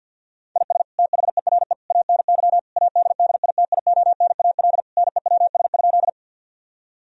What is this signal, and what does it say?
Morse code, heard as a keyed tone, about 50 wpm, 700 Hz. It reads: IS THERE ANY ADDITIONAL NEWS?
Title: Sentences from Top 1000 Words - Review Part 4 - 50wpm